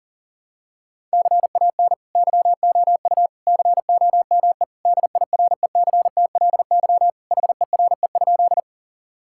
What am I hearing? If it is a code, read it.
Morse code, heard as a keyed tone, about 40 wpm, 700 Hz. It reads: CAN YOU COME DIRECTLY HERE?
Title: Sentences from Top 1000 Words - Review Part 5 - 40wpm